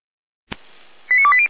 Five-tone selective call (ZVEI), heard as quick runs of five tones, tones of about 70 ms. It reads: 89280